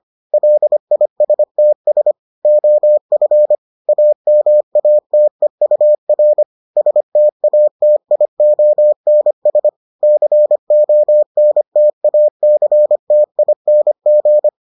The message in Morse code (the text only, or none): LISTS OF AMATEUR STATIONS CONTACTING